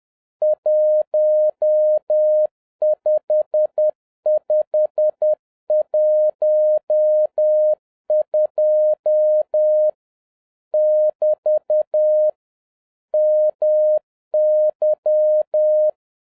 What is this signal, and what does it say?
Morse code, heard as a keyed tone, about 10 wpm, 610 Hz. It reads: 15512 <BT> MY